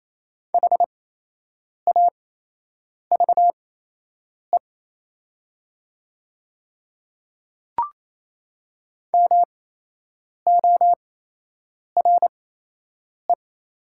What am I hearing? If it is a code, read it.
Morse code, heard as a keyed tone, about 28 wpm, spaced as for 8 wpm, 700 Hz. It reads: HAVE MORE
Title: 8x Character Spacing - Top 100 Words - 28wpm